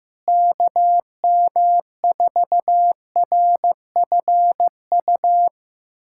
Morse code, heard as a keyed tone, about 15 wpm, 700 Hz. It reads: KM4RFU
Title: USA Callsign DE Callsign 15wpm